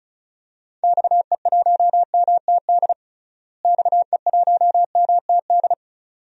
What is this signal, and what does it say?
Morse code, heard as a keyed tone, about 36 wpm, 700 Hz. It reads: XE1MTD XE1MTD
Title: DXCC-100 Call Signs - 6 Characters - Repeated 2x - 35wpm